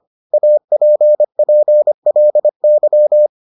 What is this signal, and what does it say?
Morse code, heard as a keyed tone, about 25 wpm, 600 Hz. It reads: APPLY